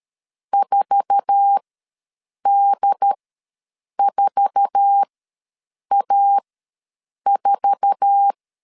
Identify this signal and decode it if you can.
Morse code, heard as a keyed tone, about 13 wpm, spaced as for 7 wpm, 800 Hz. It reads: 4D4A4
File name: LEZIONI / 8 YJ45 / 8 YJ4 ws 12wpm 3249